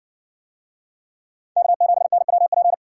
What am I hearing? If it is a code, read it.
Morse code, heard as a keyed tone, about 60 wpm, 700 Hz. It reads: K7NWP